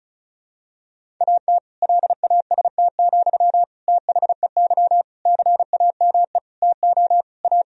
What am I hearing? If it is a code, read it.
Morse code, heard as a keyed tone, about 36 wpm, 700 Hz. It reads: AT LAST, THEY CAME TO A
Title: Sentences from Top 1000 Words - Review Part 1 - 35wpm